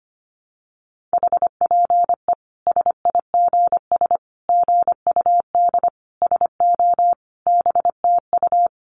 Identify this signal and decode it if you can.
Morse code, heard as a keyed tone, about 25 wpm, 700 Hz. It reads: HPE SIGS GUD SO BTU
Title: Common QSO Phrases 25wpm